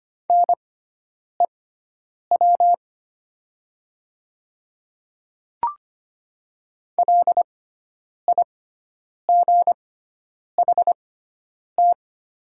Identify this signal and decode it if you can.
Morse code, heard as a keyed tone, about 25 wpm, spaced as for 9 wpm, 700 Hz. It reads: NEW LIGHT
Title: 6x Character Spacing - Top 200 Words - 25wpm